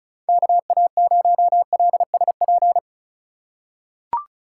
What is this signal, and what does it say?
Morse code, heard as a keyed tone, about 34 wpm, 700 Hz. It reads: KA0LSP
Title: USA Call Signs - 6 Characters - 2x Character Spacing - 35wpm